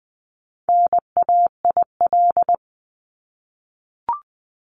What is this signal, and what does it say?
Morse code, heard as a keyed tone, about 20 wpm, 700 Hz. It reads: NAIL
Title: Top 1501-1750 Words 20wpm